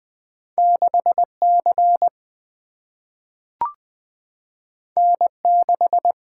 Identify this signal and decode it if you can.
Morse code, heard as a keyed tone, about 20 wpm, 700 Hz. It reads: 6 C N 6
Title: Single-Letter Single-Number 20wpm